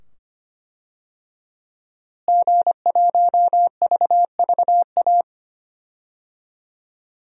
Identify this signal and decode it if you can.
Morse code, heard as a keyed tone, about 25 wpm, 700 Hz. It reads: G1VVA
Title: DXCC-100 Call Signs - 5 Characters - Repeated 2x - 25wpm